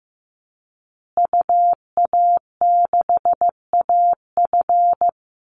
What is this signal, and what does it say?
Morse code, heard as a keyed tone, about 15 wpm, 700 Hz. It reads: UA6AF
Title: Call Signs Repeated 3x - 15wpm